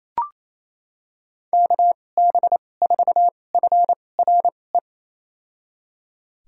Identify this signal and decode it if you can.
Morse code, heard as a keyed tone, about 28 wpm, spaced as for 20 wpm, 700 Hz. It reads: KB4FRE
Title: USA Call Signs - 6 Characters - 2x Character Spacing - 28wpm